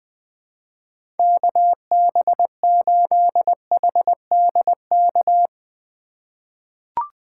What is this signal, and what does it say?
Morse code, heard as a keyed tone, about 20 wpm, 700 Hz. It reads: KB8HDK
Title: USA Call Signs - 6 Characters - 6x Character Spacing - 20wpm